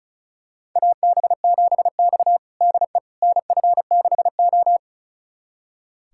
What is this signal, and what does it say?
Morse code, heard as a keyed tone, about 34 wpm, 700 Hz. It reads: AB7X DE NF6O
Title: USA Callsign DE Callsign - 4 Characters - 35wpm